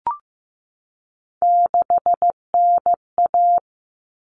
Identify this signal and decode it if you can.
Morse code, heard as a keyed tone, about 15 wpm, 700 Hz. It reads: 6NA
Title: Common Cut Numbers - 3-Digit 15wpm